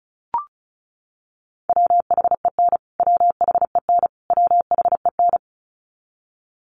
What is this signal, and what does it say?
Morse code, heard as a keyed tone, about 34 wpm, 700 Hz. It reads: WHEN WHEN WHEN
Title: Top 300 Words Repeated 35wpm